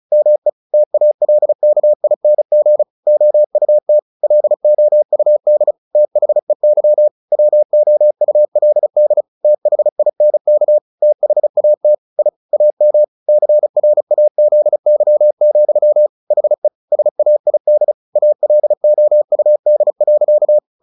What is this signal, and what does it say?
Morse code, heard as a keyed tone, about 34 wpm, 600 Hz. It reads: ME TALKING OUT LOUD THEY WOULD THINK THAT I AM CRAZY, HE SAID ALOUD.
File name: the old man and the sea / the old man and the sea 1 35wpm